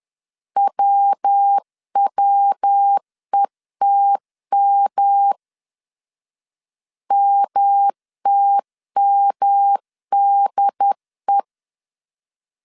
Morse code, heard as a keyed tone, about 10 wpm, 800 Hz. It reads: WWETM MTMDE
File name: 10wpm